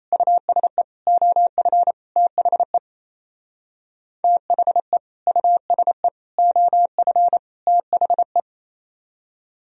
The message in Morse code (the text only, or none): USE OF THE THE USE OF THE